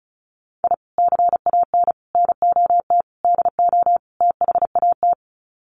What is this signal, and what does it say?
Morse code, heard as a keyed tone, about 34 wpm, 700 Hz. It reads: I CAN NOT DO THAT
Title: Sets of 5 Words Encore2 35wpm